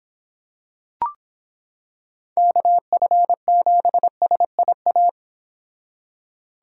Morse code, heard as a keyed tone, about 25 wpm, 700 Hz. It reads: KF7SIA